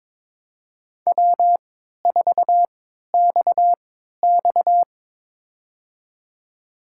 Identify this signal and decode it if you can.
Morse code, heard as a keyed tone, about 22 wpm, spaced as for 12 wpm, 700 Hz. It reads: W4XX
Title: USA Call Signs - 4 Characters - 3x Character Spacing - 22wpm